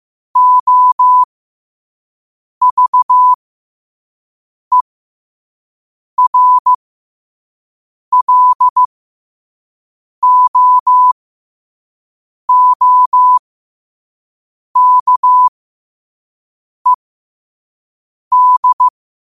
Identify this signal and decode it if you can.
Morse code, heard as a keyed tone, about 15 wpm, spaced as for 5 wpm, 1000 Hz. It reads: OVERLOOKED
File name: qotdfiles / 5wpm